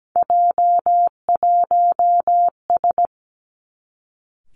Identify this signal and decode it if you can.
Morse code, heard as a keyed tone, about 17 wpm, 700 Hz. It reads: J1S
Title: Three Challenging Letter-Number Combinations 17wpm